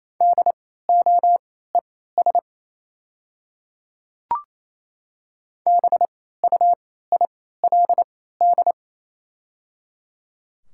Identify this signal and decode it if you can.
Morse code, heard as a keyed tone, about 28 wpm, spaced as for 16 wpm, 700 Hz. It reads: DOES BUILD
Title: 3x Character Spacing - Top 300 Words - 28wpm